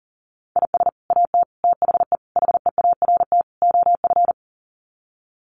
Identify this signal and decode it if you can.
Morse code, heard as a keyed tone, about 40 wpm, 700 Hz. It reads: IS AT THE HEART OF
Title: Sets of 5 Words Encore2 40wpm